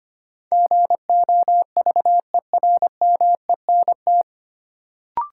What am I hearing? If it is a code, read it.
Morse code, heard as a keyed tone, about 25 wpm, 700 Hz. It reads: GOVERMENT
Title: Sentences from Top 1300 Words - New Word Drill - 25wpm